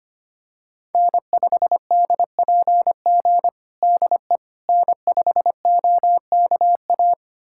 Morse code, heard as a keyed tone, about 25 wpm, 700 Hz. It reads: N5DPG DE N5OKA
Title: USA Callsign DE Callsign - 5 Characters - 25wpm